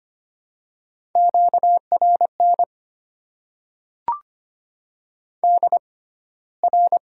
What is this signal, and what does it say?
Morse code, heard as a keyed tone, about 25 wpm, spaced as for 15 wpm, 690 Hz. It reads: QRN DR
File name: Top 90 QSO Elements - 6x Character Spacing - 25wpm